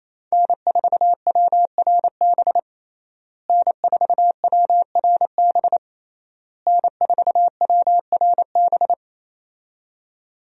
Morse code, heard as a keyed tone, about 27 wpm, 700 Hz. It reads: N4WRB N4WRB N4WRB
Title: USA Call Signs - 5 Characters - Repeated 3x - 28wpm